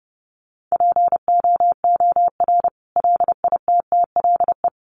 Morse code, heard as a keyed tone, about 30 wpm, 700 Hz. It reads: POOR LITTLE
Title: Sentences from Top 1000 Words 30wpm